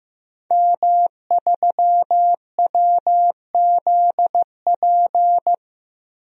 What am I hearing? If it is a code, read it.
Morse code, heard as a keyed tone, about 15 wpm, 700 Hz. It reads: M3WZP
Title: DXCC-100 Call Signs - 5 Characters - Repeated 2x - 15wpm